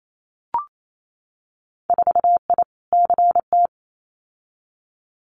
Morse code, heard as a keyed tone, about 28 wpm, 700 Hz. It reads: 4I CT
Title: Winter Field Day Exchanges Encore 28wpm